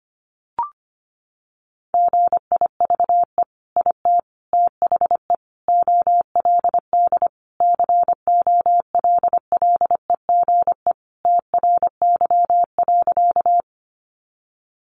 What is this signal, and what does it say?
Morse code, heard as a keyed tone, about 25 wpm, 700 Hz. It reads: GIVE IT THE OLD COLLEGE TRY.
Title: Sentences from Top 1300 Words 25wpm